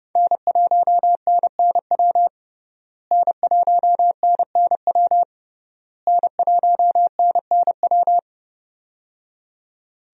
Morse code, heard as a keyed tone, about 30 wpm, 700 Hz. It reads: N1NNW N1NNW N1NNW
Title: USA Call Signs - 5 Characters - Repeated 3x - 30wpm